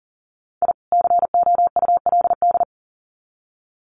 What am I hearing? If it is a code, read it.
Morse code, heard as a keyed tone, about 40 wpm, 700 Hz. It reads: I COULD
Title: Sets of 2 Words 40wpm